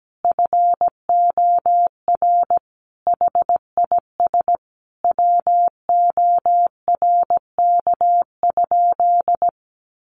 Morse code, heard as a keyed tone, about 17 wpm, 700 Hz. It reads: FOR HIS WORK?
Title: Sentences from Top 1300 Words 17wpm